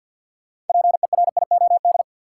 Morse code, heard as a keyed tone, about 52 wpm, 700 Hz. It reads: PERIOD
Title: Sets of 5 Words - New Word Drill - 50wpm